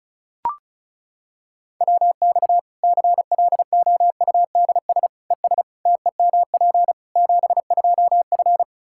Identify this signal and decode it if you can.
Morse code, heard as a keyed tone, about 34 wpm, 700 Hz. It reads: WX CLOUDS ES TEMP 72F
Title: Practice QSO - Exchange 2 - Info - 35wpm